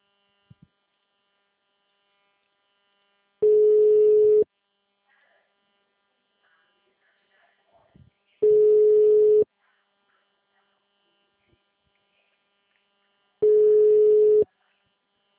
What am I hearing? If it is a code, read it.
Ringback tone (Europe, 425 Hz).
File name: out- -2014- - - .19